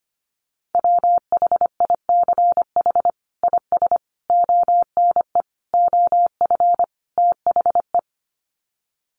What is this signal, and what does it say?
Morse code, heard as a keyed tone, about 25 wpm, 700 Hz. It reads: WHICH IS ONE OF THE
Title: Sets of 5 Words Encore 25wpm